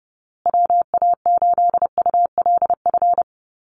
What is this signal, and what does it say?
Morse code, heard as a keyed tone, about 30 wpm, 700 Hz. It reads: WA8ULF
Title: USA Call Signs Repeated 2x 30wpm